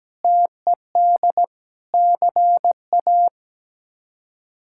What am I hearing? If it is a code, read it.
Morse code, heard as a keyed tone, about 17 wpm, 700 Hz. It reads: TED CA
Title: K1USN SST QSO Practice - 17wpm